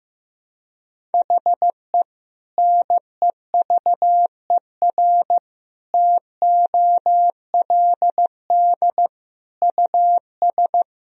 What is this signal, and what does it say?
Morse code, heard as a keyed tone, about 16 wpm, 700 Hz. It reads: HE NEVER TOLD US
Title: Sentences from Top 400 Words and Abbreviations 15wpm